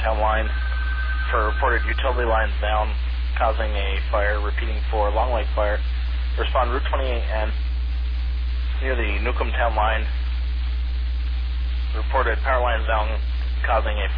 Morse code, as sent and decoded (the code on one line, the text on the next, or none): none